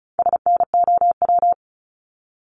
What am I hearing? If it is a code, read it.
Morse code, heard as a keyed tone, about 34 wpm, 700 Hz. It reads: SNOW
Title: Top 500 Words 35wpm